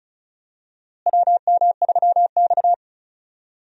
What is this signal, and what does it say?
Morse code, heard as a keyed tone, about 36 wpm, 700 Hz. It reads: WM3X
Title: USA Call Signs - 4 Characters - Repeated 3x - 35wpm